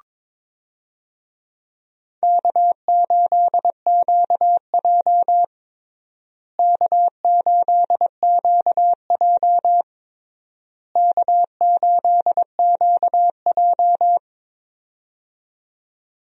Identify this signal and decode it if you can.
Morse code, heard as a keyed tone, about 22 wpm, 700 Hz. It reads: K8QJ K8QJ K8QJ
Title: USA Call Signs - 4 Characters - Repeated 3x - 22wpm